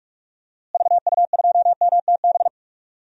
Morse code, heard as a keyed tone, about 46 wpm, 700 Hz. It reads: VU2MTB
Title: DXCC-100 Call Signs - 6 Characters - Repeated 3x - 45wpm